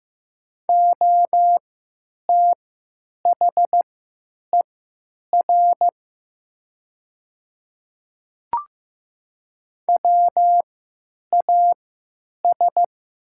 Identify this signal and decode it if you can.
Morse code, heard as a keyed tone, about 15 wpm, spaced as for 9 wpm, 700 Hz. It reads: OTHER WAS
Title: 3x Character Spacing - Top 100 Words - 15wpm